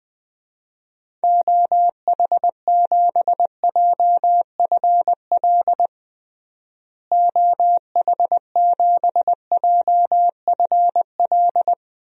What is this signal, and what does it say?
Morse code, heard as a keyed tone, about 20 wpm, 700 Hz. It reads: OH7JFL OH7JFL
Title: DXCC-100 Call Signs - 6 Characters - Repeated 3x - 20wpm